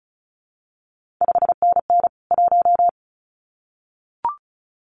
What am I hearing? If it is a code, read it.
Morse code, heard as a keyed tone, about 36 wpm, 700 Hz. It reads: 5NN 1
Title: CQ WW DX Contest Exchange Encore - 35wpm